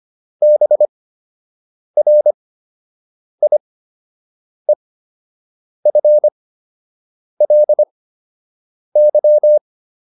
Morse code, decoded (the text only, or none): BRIEFLY